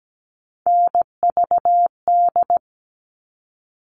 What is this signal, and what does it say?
Morse code, heard as a keyed tone, about 17 wpm, 700 Hz. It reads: NVD
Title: All Cut Numbers - 3-Digit 17wpm